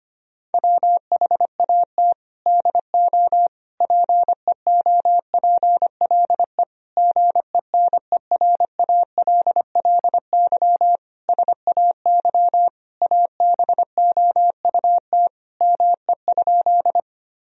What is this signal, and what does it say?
Morse code, heard as a keyed tone, about 25 wpm, 700 Hz. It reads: WHAT DO PEOPLE GENERALLY SAY ABOUT ME?
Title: Sentences from Top 1000 Words - Review Part 4 - 25wpm